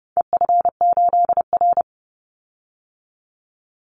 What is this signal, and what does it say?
Morse code, heard as a keyed tone, about 29 wpm, 700 Hz. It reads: EF8R
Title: Call Signs Repeated 2x - 30wpm